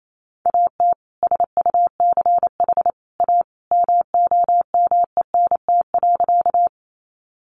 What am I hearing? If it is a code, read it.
Morse code, heard as a keyed tone, about 28 wpm, 700 Hz. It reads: AT SUCH A MOMENT.
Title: Sentences from Top 1000 Words 28wpm